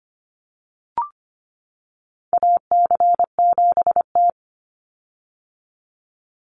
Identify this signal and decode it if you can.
Morse code, heard as a keyed tone, about 25 wpm, 700 Hz. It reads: AC7T
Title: Call Signs - 25wpm